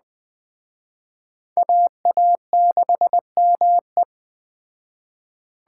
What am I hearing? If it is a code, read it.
Morse code, heard as a keyed tone, about 20 wpm, 700 Hz. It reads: AA6ME
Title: Speed Racing - USA 2x2 Call Signs 17wpm